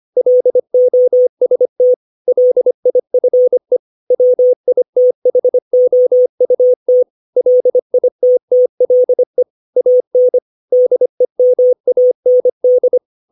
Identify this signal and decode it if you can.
Morse code, heard as a keyed tone, about 25 wpm, 500 Hz. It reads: LOST LIFE WITHOUT LITTLE AN DEMAND
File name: 632 words / 632 Words 25WPM